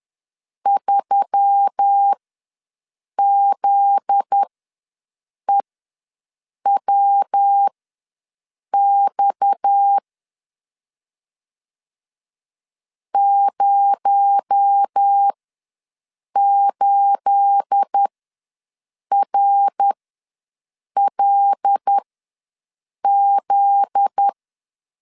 Morse code, heard as a keyed tone, about 10 wpm, spaced as for 6 wpm, 800 Hz. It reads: 3ZEWX 08RLZ
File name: LEZIONI / 9 2867 ws 10wpm 4215